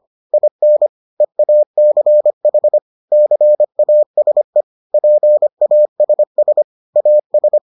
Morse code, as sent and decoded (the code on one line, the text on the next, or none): .. -. / . .- -.-. .... / -.-. .- ... / .--. .- ... ... / .- ...
IN EACH CASE PASS AS